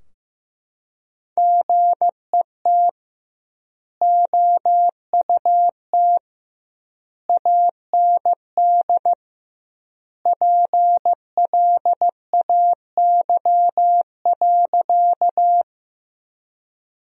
Morse code, heard as a keyed tone, about 15 wpm, 700 Hz. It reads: GET OUT AND PLAY.
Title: 2x Word-Spacing - Sentences from Top 200 Words - 15wpm